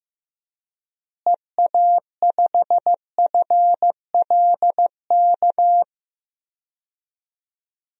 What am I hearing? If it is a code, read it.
Morse code, heard as a keyed tone, about 15 wpm, 700 Hz. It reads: EA5FLK
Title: DXCC-100 Call Signs - 6 Characters - Repeated 3x - 15wpm